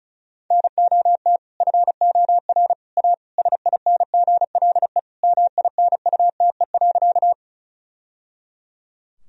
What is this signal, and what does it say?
Morse code, heard as a keyed tone, about 36 wpm, 700 Hz. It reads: NOT FOR A SINGLE MINUTE.
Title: Sentences from Top 1000 Words - Review Part 4 - 35wpm